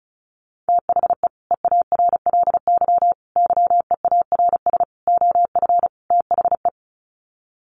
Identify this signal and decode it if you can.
Morse code, heard as a keyed tone, about 34 wpm, 700 Hz. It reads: THE EARLY YEARS OF THE
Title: Sets of 5 Words Encore 35wpm